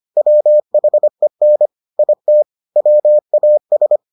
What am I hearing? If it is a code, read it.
Morse code, heard as a keyed tone, about 25 wpm, 600 Hz. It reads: WHEN IT WAS